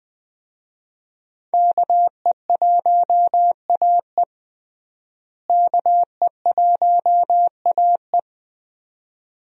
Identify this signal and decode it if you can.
Morse code, heard as a keyed tone, about 20 wpm, 700 Hz. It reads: KE1AE KE1AE
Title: USA Call Signs - 5 Characters - Repeated 2x - 20wpm